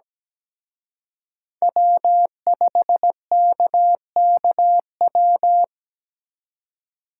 Morse code, heard as a keyed tone, about 20 wpm, 700 Hz. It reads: W5KKW
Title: Speed Racing - USA 1x3 Call Signs 17wpm